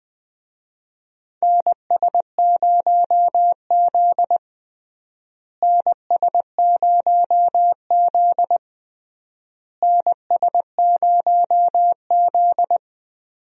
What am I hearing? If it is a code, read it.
Morse code, heard as a keyed tone, about 20 wpm, 700 Hz. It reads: NS0Z NS0Z NS0Z
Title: USA Call Signs - 4 Characters - Repeated 3x - 20wpm